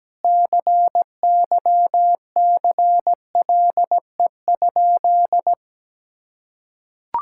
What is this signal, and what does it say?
Morse code, heard as a keyed tone, about 17 wpm, 700 Hz. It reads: CYCLE?
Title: Sentences from Top 1000 Words - Review Part 6 - 17wpm